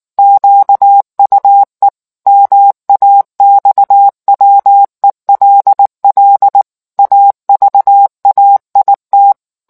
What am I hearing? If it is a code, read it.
Morse code, heard as a keyed tone, about 19 wpm, 800 Hz. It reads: QUE MAXWELL AVAIT